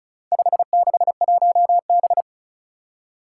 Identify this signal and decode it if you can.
Morse code, heard as a keyed tone, about 36 wpm, 700 Hz. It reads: 561B
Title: Four Challenging Letter-Number Combinations 35wpm